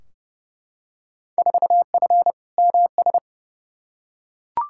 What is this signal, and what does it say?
Morse code, heard as a keyed tone, about 30 wpm, 700 Hz. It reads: 4F MS
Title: Field Day Exchanges Encore - 30wpm